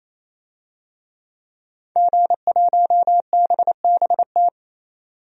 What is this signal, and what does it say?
Morse code, heard as a keyed tone, about 28 wpm, 700 Hz. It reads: G1BBT
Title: DXCC-100 Call Signs - 5 Characters - 28wpm